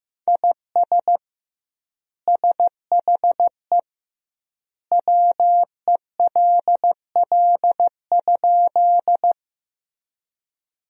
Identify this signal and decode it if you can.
Morse code, heard as a keyed tone, about 15 wpm, 700 Hz. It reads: IS SHE WELL?
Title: 2x Word-Spacing - Sentences from Top 200 Words - 15wpm